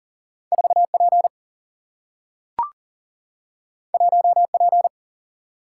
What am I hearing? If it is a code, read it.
Morse code, heard as a keyed tone, about 40 wpm, 700 Hz. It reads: 4 P 1 P
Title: Single-Letter Single-Number 40wpm